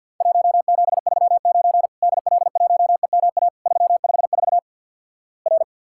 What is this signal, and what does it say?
Morse code, heard as a keyed tone, about 50 wpm, 700 Hz. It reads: 1739 DL1EMA 354 R
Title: Worked All Europe DX Contest - QTC Exchange with Repeats - 50wpm